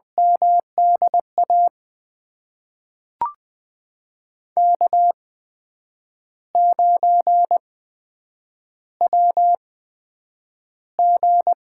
Morse code, heard as a keyed tone, about 20 wpm, spaced as for 8 wpm, 700 Hz. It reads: MDA K9WG
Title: USA Call Signs - 8x Character Spacing - 20wpm